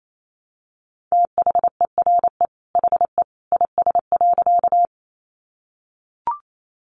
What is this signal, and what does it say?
Morse code, heard as a keyed tone, about 28 wpm, 700 Hz. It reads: THERE HE IS.
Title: A Sound Of Thunder 28wpm